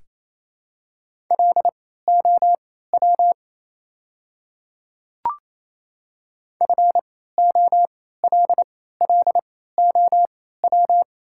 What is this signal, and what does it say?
Morse code, heard as a keyed tone, about 28 wpm, spaced as for 16 wpm, 700 Hz. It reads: LOW FOLLOW
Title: 3x Character Spacing - Top 500 Words - 28wpm